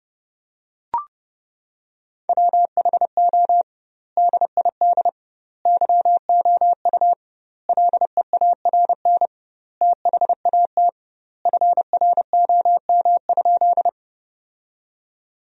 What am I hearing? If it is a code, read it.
Morse code, heard as a keyed tone, about 30 wpm, 700 Hz. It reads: WHO DID YOU LEARN THAT FROM?
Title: 2x Word-Spacing - Sentences from Top 300 Words - 30wpm